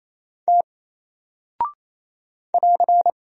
Morse code, heard as a keyed tone, about 27 wpm, 700 Hz. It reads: T <AR>
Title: Single Character (Letter:Number:Punctuation:Prosign) 28wpm